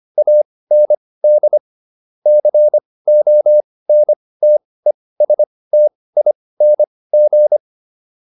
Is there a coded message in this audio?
Yes: Morse code from 0.2 to 7.6 s, heard as a keyed tone, about 25 wpm, spaced as for 18 wpm, 600 Hz. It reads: AND CONTESTING